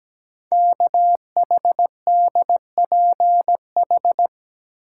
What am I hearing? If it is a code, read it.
Morse code, heard as a keyed tone, about 17 wpm, 700 Hz. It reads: KHDPH
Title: Code Groups - 5 Letters - 17wpm